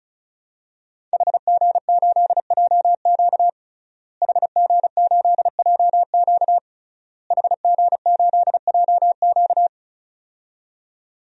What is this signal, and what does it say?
Morse code, heard as a keyed tone, about 36 wpm, 700 Hz. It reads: HG8JQ HG8JQ HG8JQ
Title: DXCC-100 Call Signs - 5 Characters - Repeated 3x - 35wpm